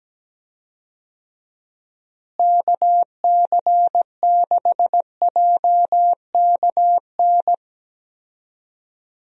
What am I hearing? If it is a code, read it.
Morse code, heard as a keyed tone, about 17 wpm, 700 Hz. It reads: KC6JKN
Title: USA Call Signs - 6 Characters - Repeated 2x - 17wpm